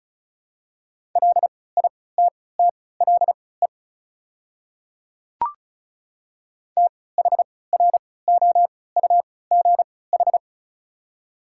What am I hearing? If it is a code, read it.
Morse code, heard as a keyed tone, about 36 wpm, spaced as for 20 wpm, 700 Hz. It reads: LITTLE THROUGH
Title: 3x Character Spacing - Top 500 Words - 35wpm